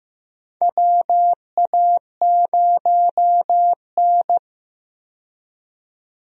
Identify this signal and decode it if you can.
Morse code, heard as a keyed tone, about 15 wpm, 700 Hz. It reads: WA0N
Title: USA Call Signs - 4 Characters - 15wpm